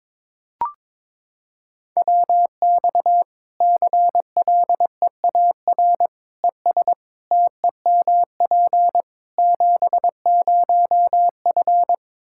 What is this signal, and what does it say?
Morse code, heard as a keyed tone, about 22 wpm, 700 Hz. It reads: WX CLEAR ES TEMP 70F
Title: Practice QSO - Exchange 2 - Info - 22wpm